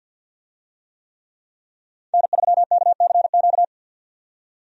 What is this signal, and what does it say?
Morse code, heard as a keyed tone, about 51 wpm, 700 Hz. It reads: N3KCQ
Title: USA Call Signs - 5 Characters - 50wpm